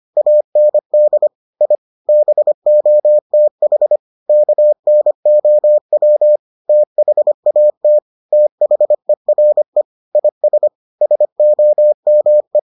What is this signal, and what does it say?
Morse code, heard as a keyed tone, about 25 wpm, 600 Hz. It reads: AND I BOTH KNOW THAT THERE IS SOME